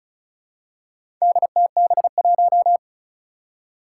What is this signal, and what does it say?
Morse code, heard as a keyed tone, about 36 wpm, 700 Hz. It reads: DTB1
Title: Four Challenging Letter-Number Combinations 35wpm